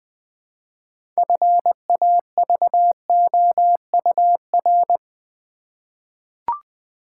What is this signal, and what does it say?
Morse code, heard as a keyed tone, about 20 wpm, 700 Hz. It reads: FAVOUR
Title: 6 Letter Words - First 3 Letters Given - 20wpm